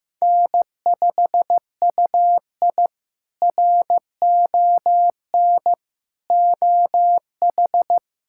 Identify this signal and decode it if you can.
Morse code, heard as a keyed tone, about 15 wpm, 700 Hz. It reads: N5UI RON OH
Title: K1USN SST QSO Practice - 15wpm